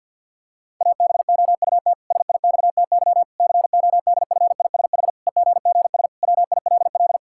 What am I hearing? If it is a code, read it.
Morse code, heard as a keyed tone, about 53 wpm, 700 Hz. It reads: ABOUT SIXTY CODFISH EGGS WILL
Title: Pangram Challenge v2 - 50wpm